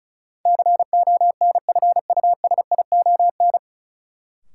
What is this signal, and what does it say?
Morse code, heard as a keyed tone, about 34 wpm, 700 Hz. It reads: CONFUSION
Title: Sentences from Top 1600 Words - New Word Drill - 35wpm